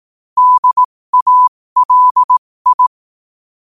Morse code, heard as a keyed tone, about 18 wpm, spaced as for 15 wpm, 1000 Hz. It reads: DALI
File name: qotdfiles / 15wpm